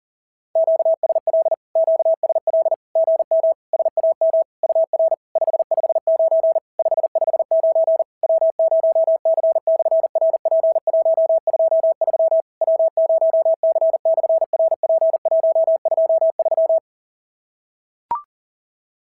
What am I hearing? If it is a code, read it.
Morse code, heard as a keyed tone, about 40 wpm, 650 Hz. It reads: QSL QSL GM SAM UR 559 559 W0C/RP123 W0C/RP123
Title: SOTA QSO Practice 40wpm